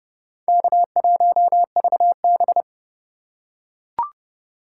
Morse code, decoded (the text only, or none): K1VB